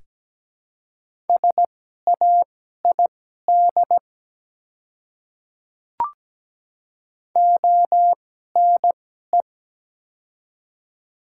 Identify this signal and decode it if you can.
Morse code, heard as a keyed tone, about 17 wpm, spaced as for 12 wpm, 700 Hz. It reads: SAID ONE
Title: 2x Character Spacing - Top 200 Words - 17wpm